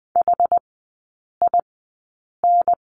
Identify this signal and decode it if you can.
Morse code, heard as a keyed tone, about 20 wpm, spaced as for 8 wpm, 700 Hz. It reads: HIN
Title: 060 Lesson 11e - Letter h - Three Characters 20wpm